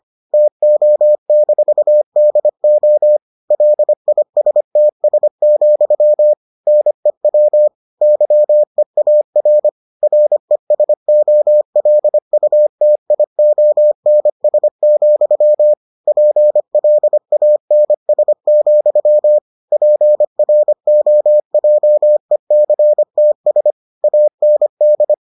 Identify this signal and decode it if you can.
Morse code, heard as a keyed tone, about 25 wpm, 600 Hz. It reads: TO-DO LISTS, NEW YEAR RESOLUTIONS, PLANS, PROJECTS AND